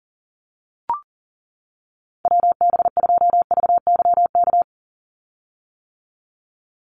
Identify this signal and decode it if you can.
Morse code, heard as a keyed tone, about 37 wpm, 700 Hz. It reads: WB2VYK
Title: Call Signs - 40wpm